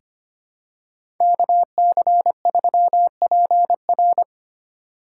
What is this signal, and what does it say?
Morse code, heard as a keyed tone, about 25 wpm, 700 Hz. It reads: KC3PR
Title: USA Call Signs - 5 Characters - 4x Character Spacing - 25wpm